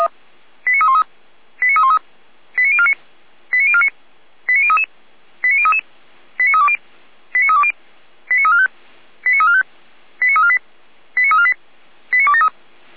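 Five-tone selective call (ZVEI), heard as quick runs of five tones, tones of about 70 ms. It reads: 89313 89313 89049 89049 89033 89033 89320 89320 89345 89345 89348 89348 89273
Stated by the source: ZVEI five-tone call received on an SDR, recorded by the Feuerwehr Orlamünde fire brigade.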